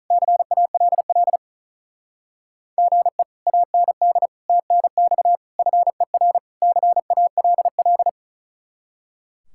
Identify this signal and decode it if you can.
Morse code, heard as a keyed tone, about 36 wpm, 700 Hz. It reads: CALL GE AND TNX FER CALL
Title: Speed-Racing Common QSO Phrases - 35wpm